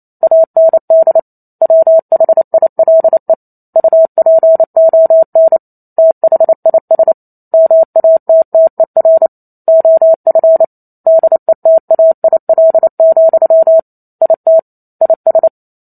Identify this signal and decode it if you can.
Morse code, heard as a keyed tone, about 28 wpm, 650 Hz. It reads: AND WHILE UPON THIS MATTER OF DETAIL, IT IS
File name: Chapter 2 - 28 WPM